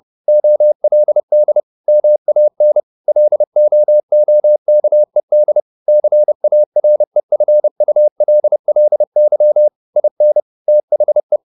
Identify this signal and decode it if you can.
Morse code, heard as a keyed tone, about 30 wpm, 600 Hz. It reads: OLD MAN LOOKED CAREFULLY IN THE